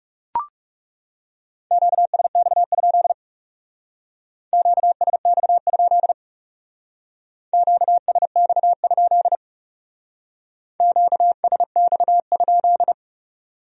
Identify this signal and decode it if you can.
Morse code, heard as a keyed tone, about 38 wpm, 700 Hz. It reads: QSX? QSX? QSX? QSX?